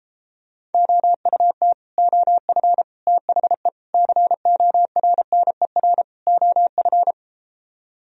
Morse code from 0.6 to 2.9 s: OUT OF

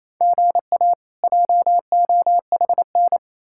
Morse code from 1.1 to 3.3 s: JOHN

